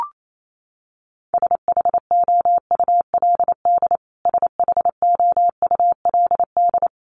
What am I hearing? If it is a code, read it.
Morse code, heard as a keyed tone, about 28 wpm, 700 Hz. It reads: SHOULD SHOULD
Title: Top 300 Words Repeated 28wpm